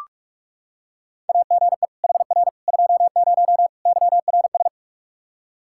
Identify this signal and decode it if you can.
Morse code, heard as a keyed tone, about 46 wpm, 700 Hz. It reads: AGE HR 20 YRS
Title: Practice QSO - Exchange 3 - More Info - 45wpm